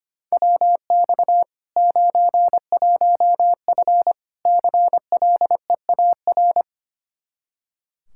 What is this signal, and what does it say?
Morse code, heard as a keyed tone, about 25 wpm, 700 Hz. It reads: WX 91F CLEAR
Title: K1USN SST QSO Practice 25wpm